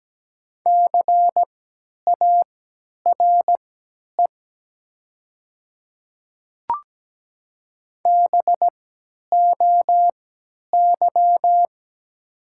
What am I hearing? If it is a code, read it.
Morse code, heard as a keyed tone, about 17 wpm, spaced as for 10 wpm, 700 Hz. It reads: CARE BOY
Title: 3x Character Spacing - Top 500 Words - 17wpm